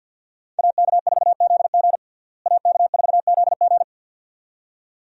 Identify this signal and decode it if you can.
Morse code, heard as a keyed tone, about 49 wpm, 700 Hz. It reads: AK4ZG AK4ZG
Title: USA Call Signs - 5 Characters - Repeated 2x - 50wpm